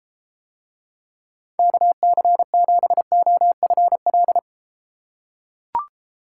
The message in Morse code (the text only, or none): KC7OFL